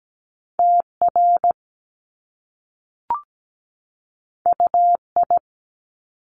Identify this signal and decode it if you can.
Morse code, heard as a keyed tone, about 17 wpm, 700 Hz. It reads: TR UI
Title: Top Two Letter Combinations in Words 17wpm